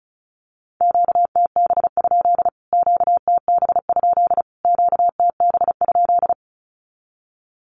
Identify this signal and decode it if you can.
Morse code, heard as a keyed tone, about 34 wpm, 700 Hz. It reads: QTB? QTB? QTB?
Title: Q-Codes Repeated 3x 35wpm